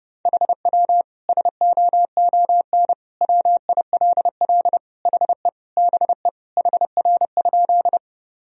Morse code, heard as a keyed tone, about 30 wpm, 700 Hz. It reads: HW SOON WILL HE BE HR?
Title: Sentences from Top 400 Words and Abbreviations 30wpm